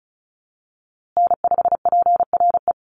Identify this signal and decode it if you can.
Morse code, heard as a keyed tone, about 34 wpm, 700 Hz. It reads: N5PRE
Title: USA Call Signs Repeated 2x 35wpm